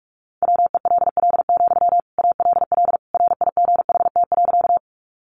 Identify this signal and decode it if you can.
Morse code, heard as a keyed tone, about 44 wpm, 700 Hz. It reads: WELL, ALL RIGHT.
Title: Sentences from Top 200 Words 45wpm